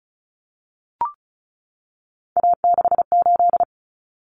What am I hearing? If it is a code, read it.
Morse code, heard as a keyed tone, about 34 wpm, 700 Hz. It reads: A68
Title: Common Cut Numbers - 3-Digit 35wpm